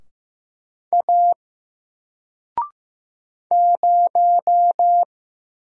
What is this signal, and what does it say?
Morse code, heard as a keyed tone, about 15 wpm, 700 Hz. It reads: A 0
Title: Single Letter-Number Alt 15wpm